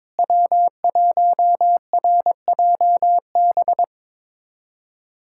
Morse code, heard as a keyed tone, about 22 wpm, 700 Hz. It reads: W1RJB
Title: USA Callsign DE Callsign - 5 Characters - 22wpm